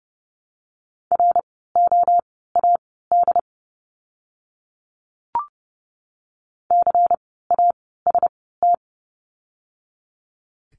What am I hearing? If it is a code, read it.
Morse code, heard as a keyed tone, about 30 wpm, spaced as for 17 wpm, 700 Hz. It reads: ROAD CAST
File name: Lesson 16f - Letter c - Words 30wpm